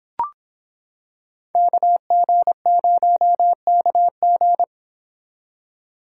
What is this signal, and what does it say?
Morse code, heard as a keyed tone, about 25 wpm, 700 Hz. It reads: KG0KG